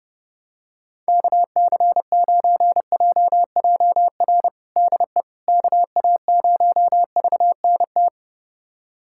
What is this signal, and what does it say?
Morse code, heard as a keyed tone, about 32 wpm, 700 Hz. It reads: KC9JJR DE KA0VNT
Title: USA Callsign DE Callsign - 6 Characters - 30wpm